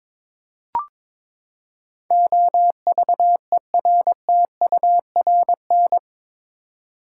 Morse code, heard as a keyed tone, about 21 wpm, 700 Hz. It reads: OVERTURN
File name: practice / 22wpm / Eight Letter Words - 22wpm